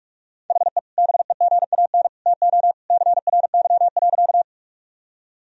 Morse code, heard as a keyed tone, about 46 wpm, 700 Hz. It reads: HE BEGAN TO CRY.
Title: Sentences from Top 1000 Words - Review Part 2 - 45wpm